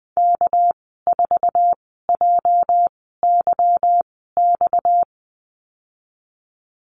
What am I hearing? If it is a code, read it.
Morse code, heard as a keyed tone, about 20 wpm, spaced as for 15 wpm, 700 Hz. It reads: K4JYX
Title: 245 Lesson 39g - Letter z - Call Signs 20wpm